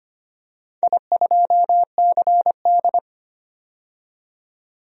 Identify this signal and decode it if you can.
Morse code, heard as a keyed tone, about 25 wpm, 700 Hz. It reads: I2CD
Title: DXCC-100 Call Signs - 4 Characters - 25wpm